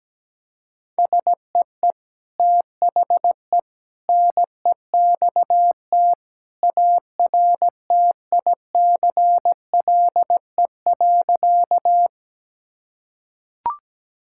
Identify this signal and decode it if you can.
Morse code, heard as a keyed tone, about 17 wpm, 700 Hz. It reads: SEE THE NEXT ARTICLE.
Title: Sentences from Top 1000 Words - Review Part 5 - 17wpm